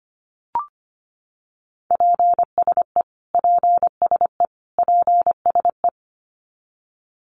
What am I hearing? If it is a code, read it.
Morse code, heard as a keyed tone, about 25 wpm, 700 Hz. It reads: PSE PSE PSE
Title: Abbreviations Repeated 3x - 25wpm